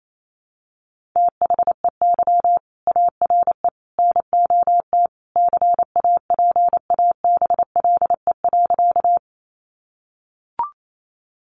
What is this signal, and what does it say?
Morse code, heard as a keyed tone, about 28 wpm, 700 Hz. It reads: THEY ARE NOT CAPABLE.
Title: Sentences from Top 1200 Words 28wpm